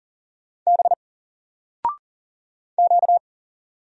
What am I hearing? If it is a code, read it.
Morse code, heard as a keyed tone, about 39 wpm, 700 Hz. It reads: B Q
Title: Single Letter-Number 40wpm